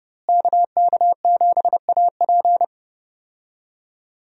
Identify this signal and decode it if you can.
Morse code, heard as a keyed tone, about 30 wpm, 700 Hz. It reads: KK7AP
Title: USA Call Signs - 5 Characters - Repeated 3x - 30wpm